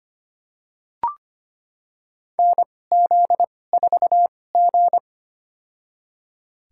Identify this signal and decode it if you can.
Morse code, heard as a keyed tone, about 25 wpm, spaced as for 18 wpm, 700 Hz. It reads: NZ4G
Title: USA Call Signs - 4 Characters - 2x Character Spacing - 25wpm